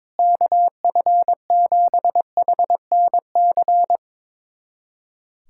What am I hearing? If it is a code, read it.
Morse code, heard as a keyed tone, about 22 wpm, 700 Hz. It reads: KF7HNC
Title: USA Callsign DE Callsign - 6 Characters - 22wpm